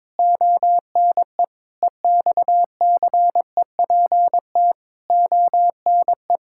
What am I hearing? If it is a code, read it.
Morse code, heard as a keyed tone, about 22 wpm, 700 Hz. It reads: ONE EXCEPT ONE
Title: Sentences from Top 1600 Words 22wpm